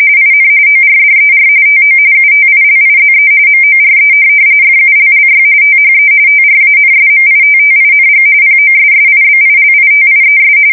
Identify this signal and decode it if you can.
RTTY, about 45.45 baud. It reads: IN ORDER TO UTILIZE MORE FULLY THE TRANSMITTING POSSIBILITIES
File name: fsk170 / printing-telegraph